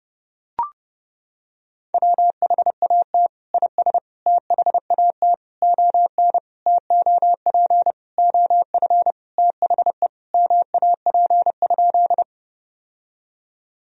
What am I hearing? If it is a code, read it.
Morse code, heard as a keyed tone, about 30 wpm, 700 Hz. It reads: WHAT IS THAT ON TOP OF THE MAP?
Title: Sentences from Top 1000 Words - Review Part 2 - 30wpm